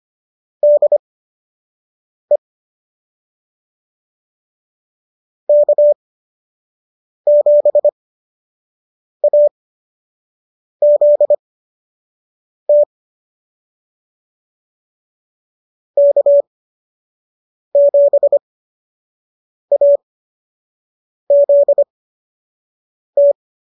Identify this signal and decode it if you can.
Morse code, heard as a keyed tone, about 25 wpm, spaced as for 6 wpm, 590 Hz. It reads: DE K7AZT K7AZT